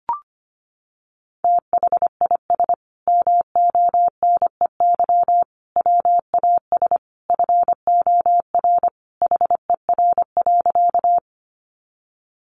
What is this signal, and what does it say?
Morse code, heard as a keyed tone, about 25 wpm, 700 Hz. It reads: THIS MONEY WAS FOR HER.
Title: Sentences from Top 400 Words 25wpm